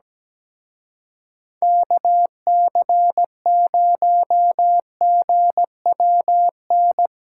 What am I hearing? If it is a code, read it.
Morse code, heard as a keyed tone, about 18 wpm, 700 Hz. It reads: KC0GWN
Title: Speed Racing - USA 2x3 Call Signs 17wpm